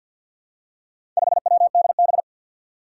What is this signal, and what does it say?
Morse code, heard as a keyed tone, about 46 wpm, 700 Hz. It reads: 5WDB
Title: Four Challenging Letter-Number Combinations 50wpm